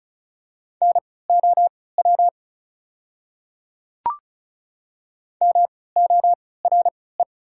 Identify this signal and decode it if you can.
Morse code, heard as a keyed tone, about 36 wpm, spaced as for 20 wpm, 700 Hz. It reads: NOW MORE
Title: 3x Character Spacing - Top 200 Words - 35wpm